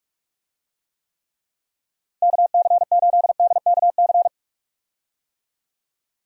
Morse code, heard as a keyed tone, about 46 wpm, 700 Hz. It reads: KC8DKC
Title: USA Call Signs - 6 Characters - Repeated 3x - 45wpm